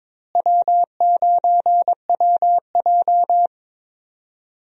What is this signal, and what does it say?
Morse code, heard as a keyed tone, about 21 wpm, 700 Hz. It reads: W9WJ
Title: USA Call Signs - 4 Characters - Repeated 3x - 22wpm